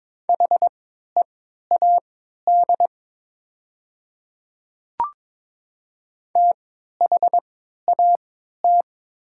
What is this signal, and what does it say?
Morse code, heard as a keyed tone, about 22 wpm, spaced as for 12 wpm, 690 Hz. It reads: HEAD THAT
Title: 3x Character Spacing - Top 300 Words - 22wpm